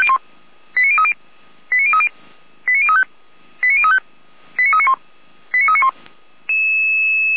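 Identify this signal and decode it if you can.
Five-tone selective call (ZVEI), heard as quick runs of five tones, tones of about 70 ms. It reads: 89030 89030 89035 89035 89391 89391
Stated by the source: Five-tone ZVEI selective call received by SDR.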